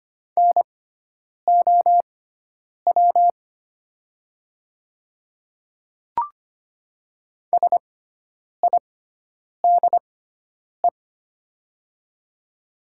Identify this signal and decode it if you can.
Morse code, heard as a keyed tone, about 25 wpm, spaced as for 9 wpm, 700 Hz. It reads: NOW SIDE
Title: 6x Character Spacing - Top 100 Words - 25wpm